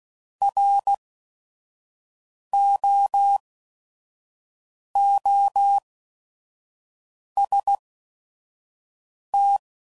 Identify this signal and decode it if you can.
Morse code, heard as a keyed tone, about 16 wpm, spaced as for 5 wpm, 800 Hz. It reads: ROOST
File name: Prefixes / By First Letter / Rs